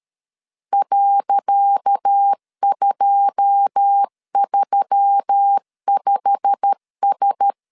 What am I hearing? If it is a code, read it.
Morse code, heard as a keyed tone, about 13 wpm, 800 Hz. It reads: .235S
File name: LEZIONI / 10 SIGNS / 10C DOT 12wpm 2451